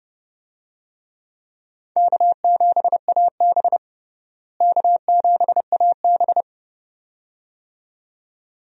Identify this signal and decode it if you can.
Morse code, heard as a keyed tone, about 30 wpm, 700 Hz. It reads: K7AB K7AB
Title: USA Call Signs - 4 Characters - Repeated 2x - 30wpm